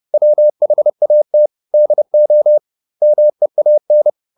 Morse code, heard as a keyed tone, about 30 wpm, 600 Hz. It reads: WHAT DO MEAN